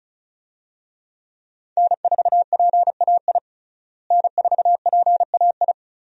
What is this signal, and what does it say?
Morse code, heard as a keyed tone, about 36 wpm, 700 Hz. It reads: N4PAI N4PAI
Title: USA Call Signs - 5 Characters - Repeated 2x - 35wpm